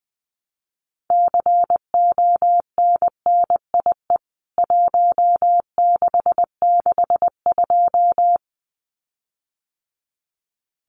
Morse code, heard as a keyed tone, about 20 wpm, 700 Hz. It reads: CONNIE 1662
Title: CWT Exchange 20wpm